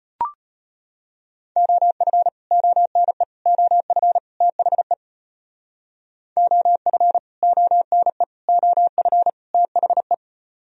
Morse code, heard as a keyed tone, about 34 wpm, 700 Hz. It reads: OF ONE OF THE OF ONE OF THE